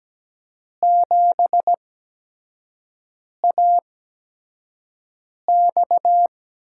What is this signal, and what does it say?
Morse code, heard as a keyed tone, about 17 wpm, spaced as for 5 wpm, 700 Hz. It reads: 7AX
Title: USA Call Signs - 6 Characters - 8x Character Spacing - 17wpm